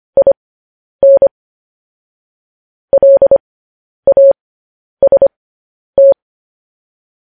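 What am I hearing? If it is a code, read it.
Morse code, heard as a keyed tone, about 25 wpm, spaced as for 10 wpm, 550 Hz. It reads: IN LAST
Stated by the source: Morse code news headlines at 10 wpm.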